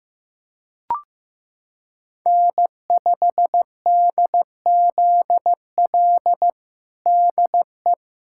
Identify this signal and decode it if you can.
Morse code, heard as a keyed tone, about 16 wpm, 700 Hz. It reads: N5DZL DE
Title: USA Callsign DE Callsign - 5 Characters - 15wpm